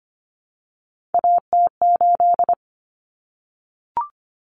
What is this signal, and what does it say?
Morse code, heard as a keyed tone, about 25 wpm, 700 Hz. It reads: AT8
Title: Common Cut Numbers - 3-Digit 25wpm